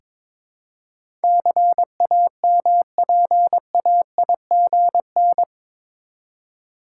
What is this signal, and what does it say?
Morse code, heard as a keyed tone, about 22 wpm, 700 Hz. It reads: CAMPAIGN
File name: practice / 22wpm / Eight Letter Words - 22wpm